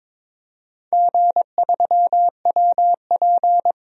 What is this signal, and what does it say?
Morse code, heard as a keyed tone, about 22 wpm, 700 Hz. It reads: G3WP
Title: DXCC-100 Call Signs - 4 Characters - 22wpm